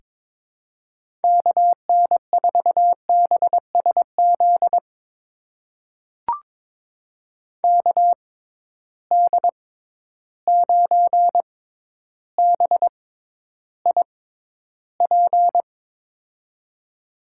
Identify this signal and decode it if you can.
Morse code, heard as a keyed tone, about 22 wpm, spaced as for 11 wpm, 700 Hz. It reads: KN4BSZ KD9BIP
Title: USA Call Signs - 6 Characters - 6x Character Spacing - 22wpm